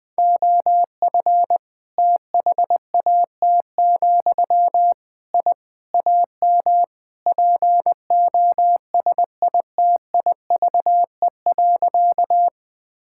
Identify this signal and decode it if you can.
Morse code, heard as a keyed tone, about 20 wpm, 700 Hz. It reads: OF THAT, I AM POSITIVE.
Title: Sentences from Top 1000 Words - Review Part 5 - 20wpm